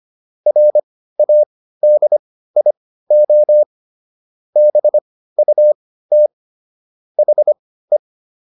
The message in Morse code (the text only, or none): RADIO BUT HE